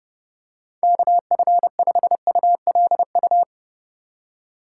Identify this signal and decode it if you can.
Morse code, heard as a keyed tone, about 30 wpm, 700 Hz. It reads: KF5ULU